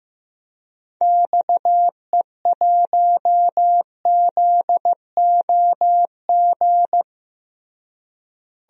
Morse code, heard as a keyed tone, about 15 wpm, 700 Hz. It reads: XE1ZOG
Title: DXCC-100 Call Signs - 6 Characters - Repeated 3x - 15wpm